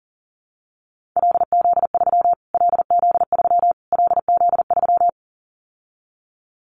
Morse code, heard as a keyed tone, about 40 wpm, 700 Hz. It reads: LZ3 LZ3 LZ3
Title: Callsign Prefixes for Top 25 DXCC - Repeated 3x - 40wpm